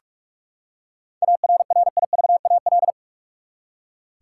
Morse code, heard as a keyed tone, about 42 wpm, 700 Hz. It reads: ARRIVAL
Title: Seven Letter Words - 45wpm